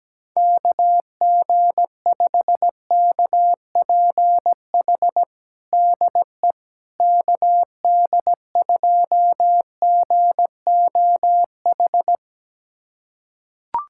Morse code, heard as a keyed tone, about 17 wpm, 700 Hz. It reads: KG5KPH DE KD2GOH
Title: USA Callsign DE Callsign 17wpm